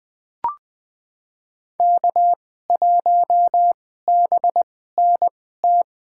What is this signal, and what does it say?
Morse code, heard as a keyed tone, about 20 wpm, spaced as for 14 wpm, 700 Hz. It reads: K1BNT
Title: USA Call Signs - 5 Characters - 2x Character Spacing - 20wpm